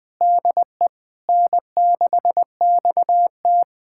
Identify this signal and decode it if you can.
Morse code, heard as a keyed tone, about 20 wpm, 700 Hz. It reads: DE N6XT
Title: USA Callsign DE Callsign - 4 Characters - 20wpm